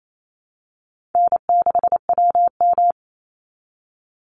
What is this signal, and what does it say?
Morse code, heard as a keyed tone, about 28 wpm, 700 Hz. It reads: N6WM
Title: Call Signs Repeated 3x - 28wpm